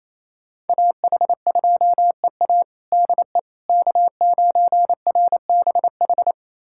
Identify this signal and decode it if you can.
Morse code, heard as a keyed tone, about 28 wpm, 700 Hz. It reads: AH2EA DE K9RBH
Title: USA Callsign DE Callsign - 5 Characters - 28wpm